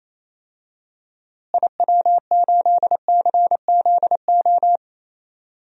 Morse code, heard as a keyed tone, about 27 wpm, 700 Hz. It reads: IW8CZO
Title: DXCC-100 Call Signs - 6 Characters - Repeated 2x - 28wpm